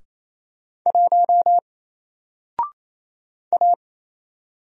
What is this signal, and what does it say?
Morse code, heard as a keyed tone, about 27 wpm, 700 Hz. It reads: 1 A
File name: Single-Letter-Number-28wpm